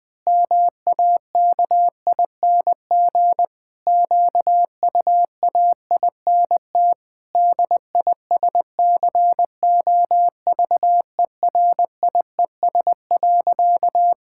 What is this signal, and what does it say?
Morse code, heard as a keyed tone, about 20 wpm, 700 Hz. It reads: MAKING QUAINT DISCOVERIES.
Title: Pangram Challenge v2 - 20wpm